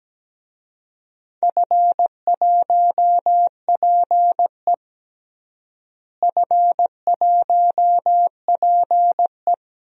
Morse code, heard as a keyed tone, about 17 wpm, 700 Hz. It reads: F1PE F1PE
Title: DXCC-100 Call Signs - 4 Characters - Repeated 3x - 17wpm